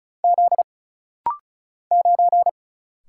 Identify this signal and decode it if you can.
Morse code, heard as a keyed tone, about 35 wpm, 700 Hz. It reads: Z 9
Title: Single Letter-Number Alt - Rapid-Fire 35wpm